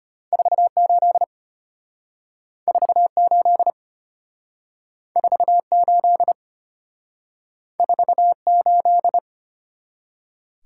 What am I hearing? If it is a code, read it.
Morse code, heard as a keyed tone, about 32 wpm, 700 Hz. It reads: 48 48 48 48